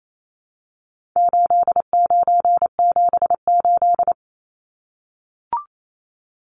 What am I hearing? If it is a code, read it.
Morse code, heard as a keyed tone, about 28 wpm, 700 Hz. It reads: 8978
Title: Numbers - 4-Digit 28wpm